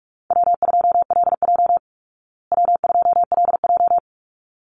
Morse code, heard as a keyed tone, about 44 wpm, 700 Hz. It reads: W2LJ W2LJ
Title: Easy Call Signs Repeated 2x - 45wpm